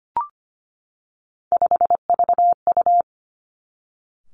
Morse code, heard as a keyed tone, about 25 wpm, 700 Hz. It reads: 5VU
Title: Three Challenging Letter-Number Combinations 25wpm